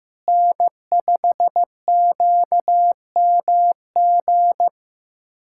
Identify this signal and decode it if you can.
Morse code, heard as a keyed tone, about 15 wpm, 700 Hz. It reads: N5QMG
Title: USA Callsign DE Callsign - 5 Characters - 15wpm